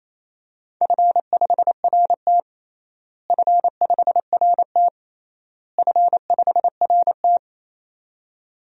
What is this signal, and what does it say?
Morse code, heard as a keyed tone, about 28 wpm, 700 Hz. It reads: F5RT F5RT F5RT
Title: DXCC-100 Call Signs - 4 Characters - Repeated 3x - 28wpm